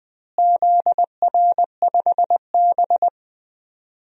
Morse code, heard as a keyed tone, about 20 wpm, 700 Hz. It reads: ZR5B
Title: DXCC-100 Call Signs - 4 Characters - Repeated 3x - 20wpm